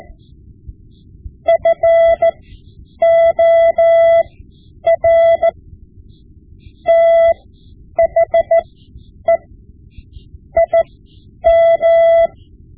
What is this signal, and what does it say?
Morse code, heard as a keyed tone, about 15 wpm, spaced as for 9 wpm, 670 Hz. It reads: FOR THE IM
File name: Morse Lessons / Morse Lessons / Tape-5B